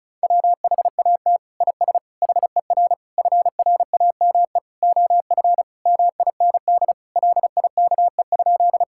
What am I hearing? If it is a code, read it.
Morse code, heard as a keyed tone, about 36 wpm, 700 Hz. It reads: WHAT IS HER FRAME OF MIND LIKE?
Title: Sentences from Top 1000 Words - Review Part 6 - 35wpm